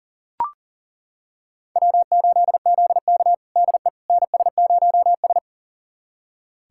Morse code, heard as a keyed tone, about 40 wpm, 700 Hz. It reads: W8ZK DE NS0S
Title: USA Callsign DE Callsign - 4 Characters - 40wpm